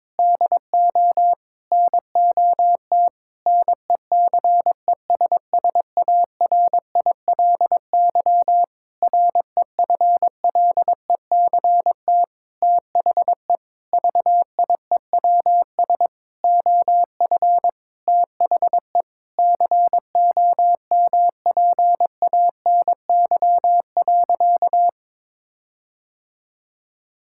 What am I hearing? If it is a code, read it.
Morse code, heard as a keyed tone, about 21 wpm, 700 Hz. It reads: DO NOT NECESSARILY REFLECT THE VIEWS OF THE COMPANY.
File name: practice / 22wpm / Sentences with 8-Word Phrases - 22wpm